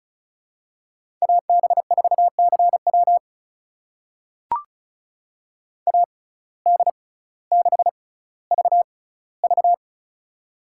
Morse code, heard as a keyed tone, about 36 wpm, spaced as for 18 wpm, 700 Hz. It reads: AB4CW AD6VV
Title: USA Call Signs - 5 Characters - 6x Character Spacing - 35wpm